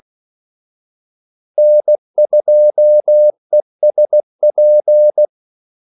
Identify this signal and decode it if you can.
Morse code, heard as a keyed tone, about 16 wpm, 600 Hz. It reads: N2ESP
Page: Kaldesignaler 16 wpm